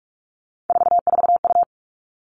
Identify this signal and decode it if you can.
Morse code, heard as a keyed tone, about 44 wpm, 700 Hz. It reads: 44U
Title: All Cut Numbers - 3-Digit 45wpm